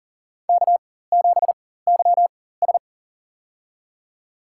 Morse code, heard as a keyed tone, about 38 wpm, spaced as for 18 wpm, 700 Hz. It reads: K7YS
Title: USA Call Signs - 4 Characters - 4x Character Spacing - 40wpm